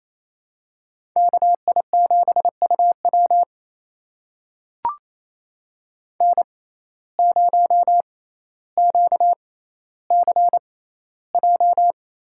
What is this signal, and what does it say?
Morse code, heard as a keyed tone, about 28 wpm, spaced as for 14 wpm, 700 Hz. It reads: KI7UW N0QCJ
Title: USA Call Signs - 5 Characters - 6x Character Spacing - 28wpm